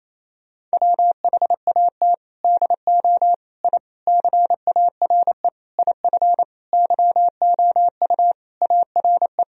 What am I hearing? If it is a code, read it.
Morse code, heard as a keyed tone, about 27 wpm, 700 Hz. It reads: WHAT DO I CARE IF YOU ARE
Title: Sentences from Top 1000 Words - Review Part 1 - 28wpm